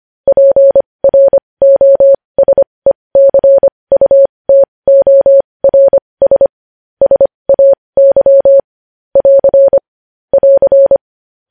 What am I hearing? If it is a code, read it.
Morse code, heard as a keyed tone, about 25 wpm, spaced as for 20 wpm, 550 Hz. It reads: PROSECUTORS SAY <AR> <AR>
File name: podcast / News headlines-1x-25x20